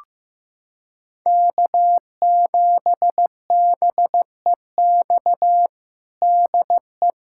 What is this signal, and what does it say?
Morse code, heard as a keyed tone, about 15 wpm, 700 Hz. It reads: K7BEX DE
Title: USA Callsign DE Callsign - 5 Characters - 15wpm